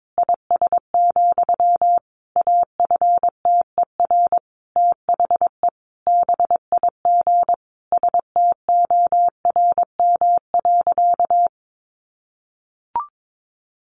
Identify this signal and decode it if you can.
Morse code, heard as a keyed tone, about 22 wpm, 700 Hz. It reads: IS, AFTER THE BIG STORM.
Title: Sentences from Top 1500 Words 22wpm